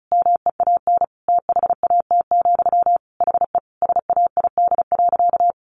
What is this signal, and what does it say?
Morse code, heard as a keyed tone, about 34 wpm, 700 Hz. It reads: MEAN THAT, HE SAID.
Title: Sentences from Top 600 Words 35wpm